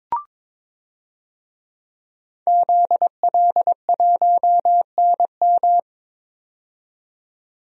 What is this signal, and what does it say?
Morse code, heard as a keyed tone, about 22 wpm, 700 Hz. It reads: ZL1NM
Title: DXCC-100 Call Signs - 5 Characters - 22wpm